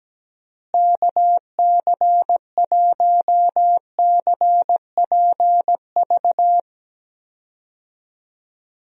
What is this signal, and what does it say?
Morse code, heard as a keyed tone, about 17 wpm, 700 Hz. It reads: KC1CPV
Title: USA Call Signs - 6 Characters - Repeated 3x - 17wpm